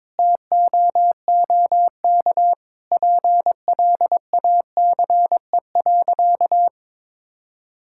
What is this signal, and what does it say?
Morse code, heard as a keyed tone, about 22 wpm, 700 Hz. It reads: TOOK PLACE.
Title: Sentences from Top 1600 Words 22wpm